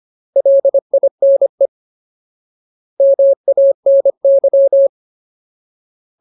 Morse code, heard as a keyed tone, about 25 wpm, 550 Hz. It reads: LINE MANY
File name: Ultimate Word List / 25WPM Ultimate